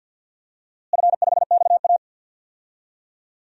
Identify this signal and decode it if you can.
Morse code, heard as a keyed tone, about 47 wpm, 700 Hz. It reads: F5XA
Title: DXCC-100 Call Signs - 4 Characters - Repeated 3x - 50wpm